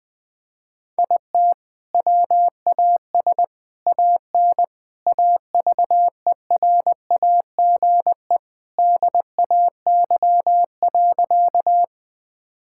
Morse code, heard as a keyed tone, about 20 wpm, 700 Hz. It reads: IT WAS AN AVERAGE DAY.
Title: Sentences from Top 1000 Words - Review Part 6 - 20wpm